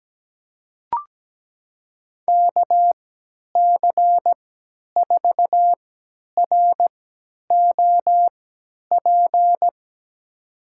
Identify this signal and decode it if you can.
Morse code, heard as a keyed tone, about 17 wpm, spaced as for 10 wpm, 700 Hz. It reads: KC4ROP
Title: USA Call Signs - 6 Characters - 3x Character Spacing - 17wpm